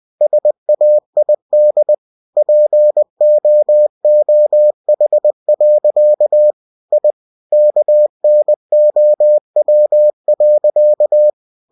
Morse code, heard as a keyed tone, about 20 wpm, 600 Hz. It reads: SAID POOH. I KNOW.